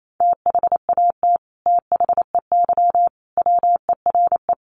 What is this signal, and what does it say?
Morse code, heard as a keyed tone, about 28 wpm, 700 Hz. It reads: THAT THEY WERE
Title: Sets of 5 Words Encore 28wpm